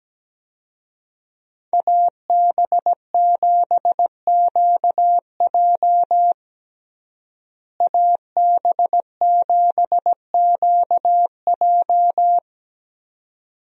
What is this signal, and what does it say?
Morse code, heard as a keyed tone, about 17 wpm, 700 Hz. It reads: AB7QJ AB7QJ
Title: USA Call Signs - 5 Characters - Repeated 2x - 17wpm